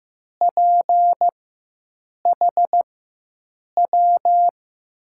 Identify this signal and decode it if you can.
Morse code, heard as a keyed tone, about 15 wpm, spaced as for 7 wpm, 700 Hz. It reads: PHW
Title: USA Call Signs - 6 Characters - 4x Character Spacing - 15wpm